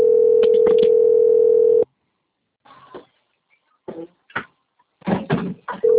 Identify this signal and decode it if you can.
Ringback tone (North America), 440 + 480 Hz.
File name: out- - - - - .66758